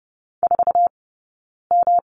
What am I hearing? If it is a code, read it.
Morse code, heard as a keyed tone, about 30 wpm, spaced as for 9 wpm, 690 Hz. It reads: 4M
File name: Lesson 19d - Letter w - Two Characters 30wpm